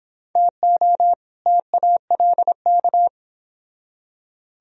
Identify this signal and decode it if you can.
Morse code, heard as a keyed tone, about 30 wpm, 700 Hz. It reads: TO TALK